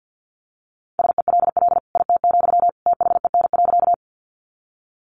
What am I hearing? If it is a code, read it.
Morse code, heard as a keyed tone, about 50 wpm, 700 Hz. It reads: SELL IT, THEN.
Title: Sentences from Top 1100 Words 50wpm